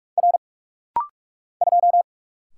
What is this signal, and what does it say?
Morse code, heard as a keyed tone, about 44 wpm, 700 Hz. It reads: R 2
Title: Single Letter-Number - Rapid-Fire 45wpm